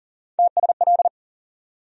Morse code, heard as a keyed tone, about 40 wpm, 700 Hz. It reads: TSL
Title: USA Call Sign Postfixes - 3 Letters 40wpm